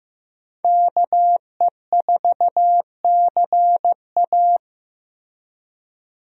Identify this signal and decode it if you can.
Morse code, heard as a keyed tone, about 15 wpm, 700 Hz. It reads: KE4CA
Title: USA Call Signs - 5 Characters - Repeated 2x - 15wpm